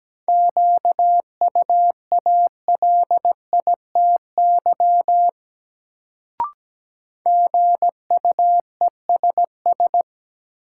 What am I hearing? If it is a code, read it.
Morse code, heard as a keyed tone, about 17 wpm, 700 Hz. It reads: QUALITY GUESS
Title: Sentences from Top 1500 Words - New Word Drill Alt - 17wpm